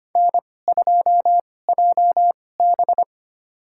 Morse code, heard as a keyed tone, about 25 wpm, spaced as for 18 wpm, 700 Hz. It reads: N2JB
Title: USA Call Signs - 4 Characters - 2x Character Spacing - 25wpm